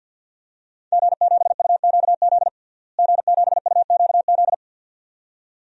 Morse code, heard as a keyed tone, about 52 wpm, 700 Hz. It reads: G7UQZ G7UQZ
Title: DXCC-100 Call Signs - 5 Characters - Repeated 2x - 50wpm